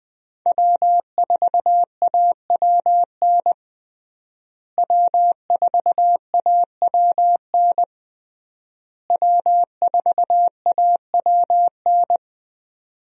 Morse code, heard as a keyed tone, about 21 wpm, 700 Hz. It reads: W4AWN W4AWN W4AWN
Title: USA Call Signs - 5 Characters - Repeated 3x - 20wpm